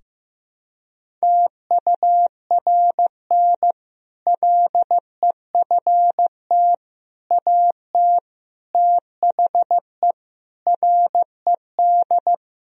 Morse code, heard as a keyed tone, about 15 wpm, 700 Hz. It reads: TURN LEFT AT THE RED
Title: Sentences from Top 1000 Words - Review Part 2 - 15wpm